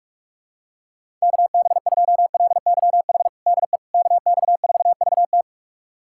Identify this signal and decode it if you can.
Morse code, heard as a keyed tone, about 46 wpm, 700 Hz. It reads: KB2LYH DE KX4VT
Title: USA Callsign DE Callsign 45wpm